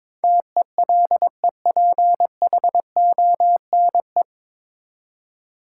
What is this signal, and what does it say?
Morse code, heard as a keyed tone, about 22 wpm, 700 Hz. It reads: TELEPHONE
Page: Morse Code Ninja practice 22wpm